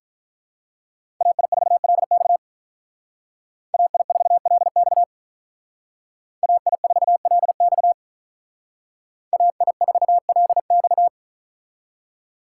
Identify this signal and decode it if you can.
Morse code, heard as a keyed tone, about 42 wpm, 700 Hz. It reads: AI4LX AI4LX AI4LX AI4LX
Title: Speed Racing - USA 2x2 Call Signs 35wpm